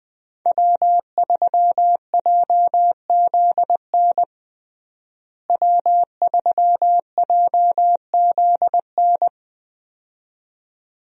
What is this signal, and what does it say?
Morse code, heard as a keyed tone, about 20 wpm, 700 Hz. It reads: W3JZN W3JZN
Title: USA Call Signs - 5 Characters - Repeated 2x - 20wpm